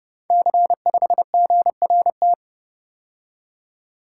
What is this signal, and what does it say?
Morse code, heard as a keyed tone, about 30 wpm, 700 Hz. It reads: C5GRT
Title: Code Groups - 5 Letter:Numbers - 30wpm